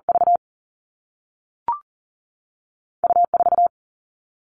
Morse code, heard as a keyed tone, about 40 wpm, 700 Hz. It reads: V U 4
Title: Two Challenging Letter-Number Combinations 40wpm